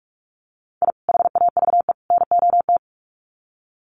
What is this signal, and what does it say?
Morse code, heard as a keyed tone, about 44 wpm, 700 Hz. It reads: I HAVE NOT
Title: Sets of 3 Words 45wpm